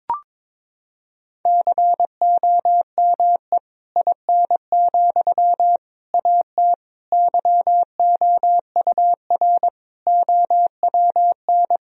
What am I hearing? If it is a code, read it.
Morse code, heard as a keyed tone, about 22 wpm, 700 Hz. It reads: COME IN, AT YOUR OWN
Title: Sentences from Top 1000 Words - Review Part 4 - 22wpm